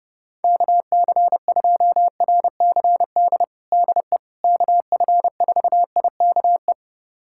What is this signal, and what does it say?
Morse code, heard as a keyed tone, about 30 wpm, 700 Hz. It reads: KC2RCD DE KF4IKE
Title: USA Callsign DE Callsign 30wpm